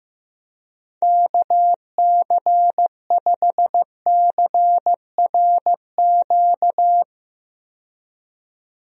Morse code, heard as a keyed tone, about 15 wpm, 700 Hz. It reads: KC5CRQ
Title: USA Call Signs - 6 Characters - 15wpm